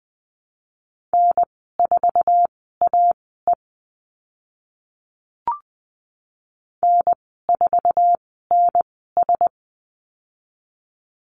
Morse code, heard as a keyed tone, about 20 wpm, spaced as for 14 wpm, 700 Hz. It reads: N4AE N4NS
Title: 062 Lesson 11g - Letter h - Call Signs 20wpm